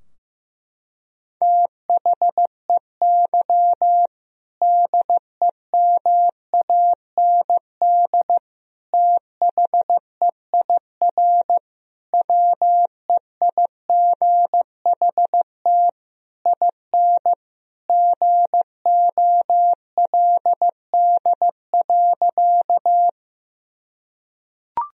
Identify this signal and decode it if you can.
Morse code, heard as a keyed tone, about 15 wpm, 700 Hz. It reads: THEY DEMAND THEIR WEIGHT IN GOLD.
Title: Sentences from Top 1000 Words - Review Part 3 - 15wpm